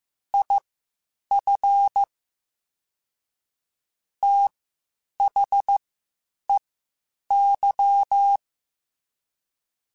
Morse code, heard as a keyed tone, about 15 wpm, spaced as for 8 wpm, 800 Hz. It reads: IF THEY
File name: Quotes Spaced / Quotes Spaced-mp3s / Advice sp @13